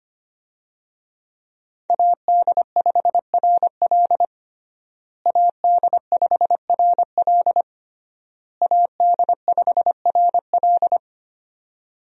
Morse code, heard as a keyed tone, about 25 wpm, 700 Hz. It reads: AD5RL AD5RL AD5RL
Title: USA Call Signs - 5 Characters - Repeated 3x - 25wpm